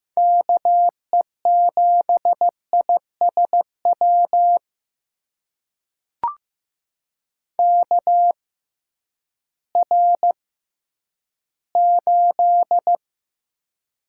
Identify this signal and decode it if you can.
Morse code, heard as a keyed tone, about 15 wpm, spaced as for 10 wpm, 700 Hz. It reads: KE7ISW KR8
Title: USA Call Signs - 6 Characters - 6x Character Spacing - 15wpm